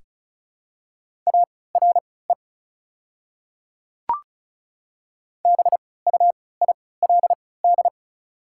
Morse code, heard as a keyed tone, about 34 wpm, spaced as for 20 wpm, 700 Hz. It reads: ARE BUILD
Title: 3x Character Spacing - Top 200 Words - 35wpm